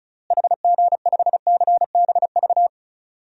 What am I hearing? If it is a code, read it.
Morse code, heard as a keyed tone, about 36 wpm, 700 Hz. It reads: HG5CBV
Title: DXCC-100 Call Signs - 6 Characters - Repeated 2x - 35wpm